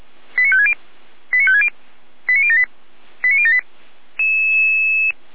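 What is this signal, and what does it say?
Five-tone selective call (ZVEI), heard as quick runs of five tones, tones of about 70 ms. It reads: 89560 89560 89078 89078
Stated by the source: ZVEI five-tone call 89099 received by SDR, from the audio server of the Feuerwehr Orlamünde fire brigade.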